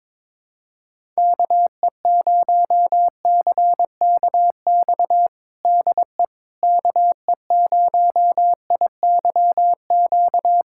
Morse code, heard as a keyed tone, about 22 wpm, 700 Hz. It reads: KE0CKX DE KE0IYQ
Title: USA Callsign DE Callsign - 6 Characters - 22wpm